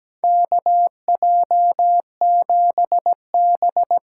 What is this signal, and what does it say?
Morse code, heard as a keyed tone, about 17 wpm, 700 Hz. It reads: KJ7B
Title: USA Call Signs - 6 Characters - 8x Character Spacing - 17wpm